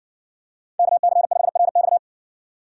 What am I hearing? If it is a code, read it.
Morse code, heard as a keyed tone, about 60 wpm, 700 Hz. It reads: KC5RX